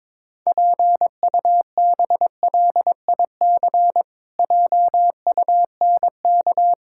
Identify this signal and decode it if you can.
Morse code, heard as a keyed tone, about 22 wpm, 700 Hz. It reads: PUBLIC JUNK
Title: Pangram Challenge v2 - 22wpm